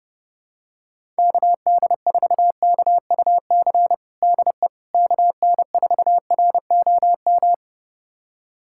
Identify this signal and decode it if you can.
Morse code, heard as a keyed tone, about 28 wpm, 700 Hz. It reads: KD4KUC DE KN4ROM
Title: USA Callsign DE Callsign 30wpm